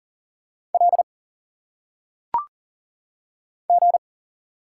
Morse code, heard as a keyed tone, about 39 wpm, 700 Hz. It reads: L G
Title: Single Letters - Phonetic Pronunciation 40wpm